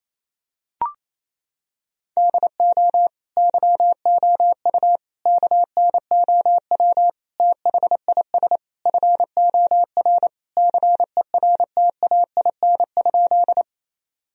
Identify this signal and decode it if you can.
Morse code, heard as a keyed tone, about 28 wpm, 700 Hz. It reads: DO YOU KNOW THIS FOR CERTAIN?
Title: Sentences from Top 1000 Words - Review Part 2 - 28wpm